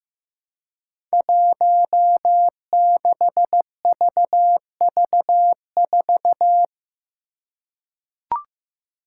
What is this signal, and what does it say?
Morse code, heard as a keyed tone, about 15 wpm, 700 Hz. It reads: 16VV4
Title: Five Challenging Letter-Number Combinations 15wpm